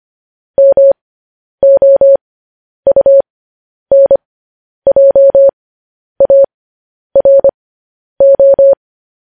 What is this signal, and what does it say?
Morse code, heard as a keyed tone, about 25 wpm, spaced as for 10 wpm, 550 Hz. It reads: MOUNJARO
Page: News Headlines in Morse Code at 10 WPM